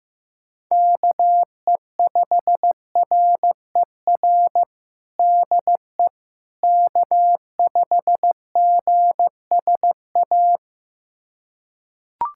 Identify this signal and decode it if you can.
Morse code, heard as a keyed tone, about 16 wpm, 700 Hz. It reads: KE5RER DE K5GSA
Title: USA Callsign DE Callsign 15wpm